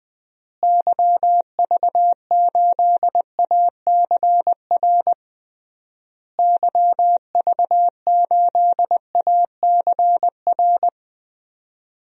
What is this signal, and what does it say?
Morse code, heard as a keyed tone, about 20 wpm, 700 Hz. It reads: YV8ACR YV8ACR
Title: DXCC-100 Call Signs - 6 Characters - Repeated 3x - 20wpm